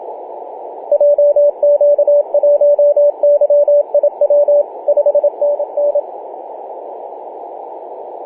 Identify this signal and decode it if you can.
Morse code, heard as a keyed tone, about 26 wpm, 580 Hz, with radio noise. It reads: JQ1YIW 5NN
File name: posts / oct-08-2033z-tx5z-21-008mhz-cw